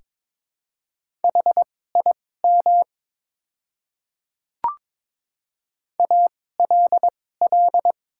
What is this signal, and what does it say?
Morse code, heard as a keyed tone, about 22 wpm, spaced as for 16 wpm, 700 Hz. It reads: HIM ALL
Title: 2x Character Spacing - Top 200 Words - 22wpm